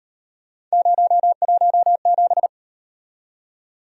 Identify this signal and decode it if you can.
Morse code, heard as a keyed tone, about 38 wpm, 700 Hz. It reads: 017